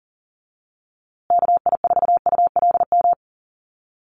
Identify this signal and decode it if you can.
Morse code, heard as a keyed tone, about 40 wpm, 700 Hz. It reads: KI4ULM
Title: USA Call Signs Repeated 3x 40wpm